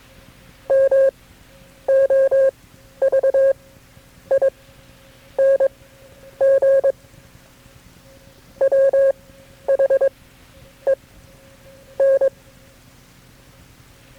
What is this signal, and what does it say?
Morse code, heard as a keyed tone, about 22 wpm, spaced as for 9 wpm, 540 Hz. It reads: MOVING WHEN
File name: KB6MT-tape3-side1